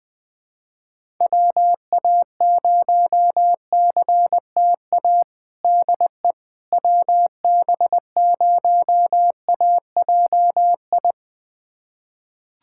Morse code, heard as a keyed tone, about 20 wpm, 700 Hz. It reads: WA0CTA DE WB0AJI
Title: USA Callsign DE Callsign - 6 Characters - 20wpm